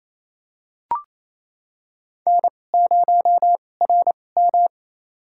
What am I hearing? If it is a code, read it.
Morse code, heard as a keyed tone, about 28 wpm, spaced as for 20 wpm, 700 Hz. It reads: N0RM
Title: USA Call Signs - 2x Character Spacing - 28wpm